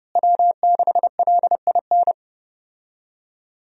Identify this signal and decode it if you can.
Morse code, heard as a keyed tone, about 30 wpm, 700 Hz. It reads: W6LIN
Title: USA Call Signs - 5 Characters - Repeated 2x - 30wpm